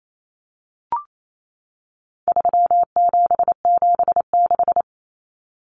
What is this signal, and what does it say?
Morse code, heard as a keyed tone, about 28 wpm, 700 Hz. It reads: 3776
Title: Numbers - 4-Digit 28wpm